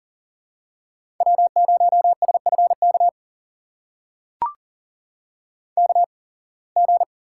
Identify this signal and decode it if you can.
Morse code, heard as a keyed tone, about 38 wpm, spaced as for 26 wpm, 700 Hz. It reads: W0SFK KG
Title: USA Call Signs - 5 Characters - 8x Character Spacing - 40wpm